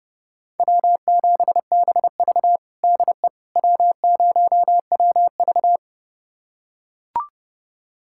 Morse code, heard as a keyed tone, about 30 wpm, 700 Hz. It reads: W7BV DE W0WV
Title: USA Callsign DE Callsign - 4 Characters - 30wpm